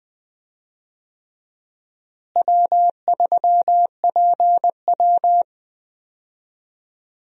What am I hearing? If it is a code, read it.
Morse code, heard as a keyed tone, about 21 wpm, 700 Hz. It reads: W3PW
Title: USA Call Signs - 4 Characters - Repeated 3x - 20wpm